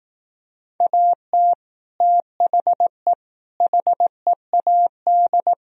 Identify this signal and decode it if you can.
Morse code, heard as a keyed tone, about 18 wpm, 700 Hz. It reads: AT THE HEAD
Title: Speed-Racing - Sets of 5 Words 15wpm